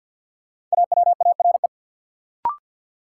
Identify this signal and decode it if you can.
Morse code, heard as a keyed tone, about 53 wpm, 700 Hz. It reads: AWARE
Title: Sentences from Top 1200 Words - New Word Drill Alt - 50wpm